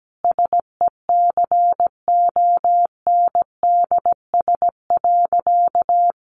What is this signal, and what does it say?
Morse code, heard as a keyed tone, about 17 wpm, 700 Hz. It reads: SECONDS.
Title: A Sound Of Thunder 17wpm